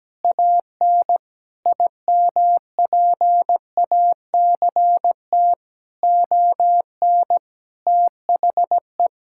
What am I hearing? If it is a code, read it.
Morse code, heard as a keyed tone, about 17 wpm, 700 Hz. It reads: AN IMPACT ON THE
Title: Sentences from Top 1000 Words - Review Part 6 - 17wpm